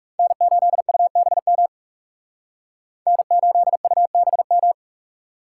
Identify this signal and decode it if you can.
Morse code, heard as a keyed tone, about 38 wpm, 700 Hz. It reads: N8UBM N8UBM